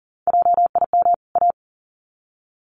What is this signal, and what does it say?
Morse code, heard as a keyed tone, about 40 wpm, 700 Hz. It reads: JIM A
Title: CWT Exchange 40wpm